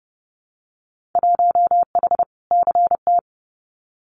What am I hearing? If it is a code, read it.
Morse code, heard as a keyed tone, about 30 wpm, 700 Hz. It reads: 1H CT
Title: Winter Field Day Exchanges 30wpm